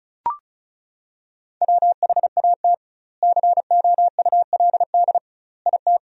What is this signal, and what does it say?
Morse code, heard as a keyed tone, about 34 wpm, 700 Hz. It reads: WHAT COULD IT
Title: 2x Word-Spacing - Sentences from Top 100 Words - 35wpm